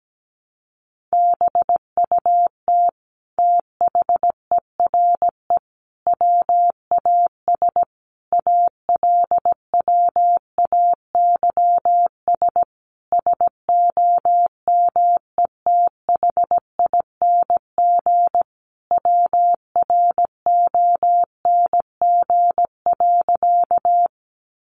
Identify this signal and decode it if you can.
Morse code, heard as a keyed tone, about 17 wpm, 700 Hz. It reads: BUT THERE WAS ALWAYS SOMETHING WRONG.
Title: Sentences from Top 900 Words 17wpm